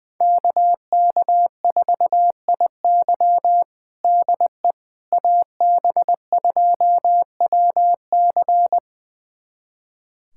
Morse code, heard as a keyed tone, about 21 wpm, 700 Hz. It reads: KK4IY DE AB2WC
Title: USA Callsign DE Callsign - 5 Characters - 20wpm